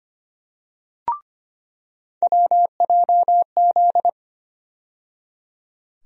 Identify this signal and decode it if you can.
Morse code, heard as a keyed tone, about 25 wpm, 700 Hz. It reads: WJZ
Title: USA Call Sign Postfixes - 3 Letters 25wpm